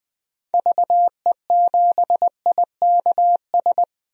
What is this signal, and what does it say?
Morse code, heard as a keyed tone, about 20 wpm, 700 Hz. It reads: VE7IKS
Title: DXCC-100 Call Signs - 6 Characters - 20wpm